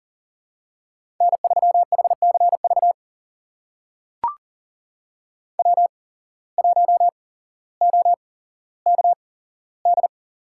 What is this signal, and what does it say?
Morse code, heard as a keyed tone, about 40 wpm, spaced as for 17 wpm, 700 Hz. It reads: N3HCV W1OKD
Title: USA Call Signs - 5 Characters - 8x Character Spacing - 40wpm